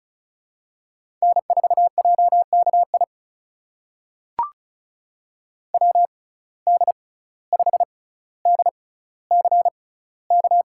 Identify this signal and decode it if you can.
Morse code, heard as a keyed tone, about 36 wpm, spaced as for 17 wpm, 700 Hz. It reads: N4JKI WD5DCK
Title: USA Call Signs - 6x Character Spacing - 35wpm